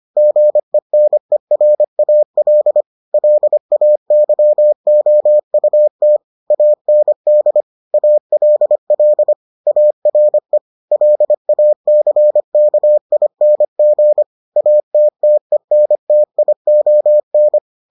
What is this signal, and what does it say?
Morse code, heard as a keyed tone, about 25 wpm, 600 Hz. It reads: GENERAL LAYOUT AND ALL ARE LACKING ATTENTION